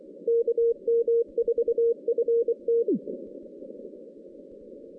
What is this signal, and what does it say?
Morse code, heard as a keyed tone, about 24 wpm, 460 Hz. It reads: KM4FT
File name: 2016 JAN CW NAQP / 2016 Jan naqp k5ot - 0-11